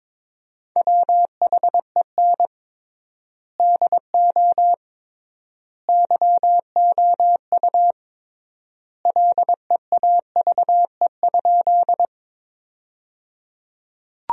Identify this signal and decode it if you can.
Morse code, heard as a keyed tone, about 22 wpm, 700 Hz. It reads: WHEN DO YOU LEAVE?
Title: 3x Word-Spacing - Sentences from Top 400 Words - 22wpm